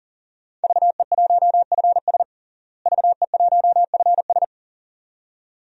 Morse code, heard as a keyed tone, about 40 wpm, 700 Hz. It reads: VE1FS VE1FS
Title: DXCC-100 Call Signs - 5 Characters - Repeated 2x - 40wpm